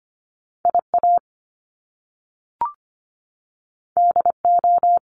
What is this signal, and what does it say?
Morse code, heard as a keyed tone, about 25 wpm, 700 Hz. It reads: IA DO
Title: Top Two Letter Combinations in Words 25wpm